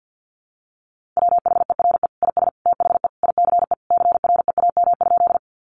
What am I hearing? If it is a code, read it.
Morse code, heard as a keyed tone, about 50 wpm, 700 Hz. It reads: WHERE IS THE ICE CREAM?
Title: Sentences from Top 1500 Words 50wpm